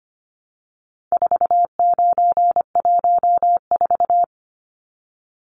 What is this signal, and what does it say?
Morse code, heard as a keyed tone, about 25 wpm, 700 Hz. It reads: 4914
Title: Numbers - 4-Digit 25wpm